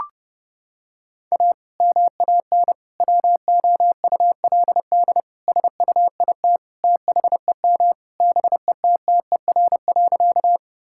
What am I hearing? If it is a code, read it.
Morse code, heard as a keyed tone, about 31 wpm, 700 Hz. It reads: A MAN WOULD SUIT THEM BETTER.
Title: Sentences from Top 1600 Words 30wpm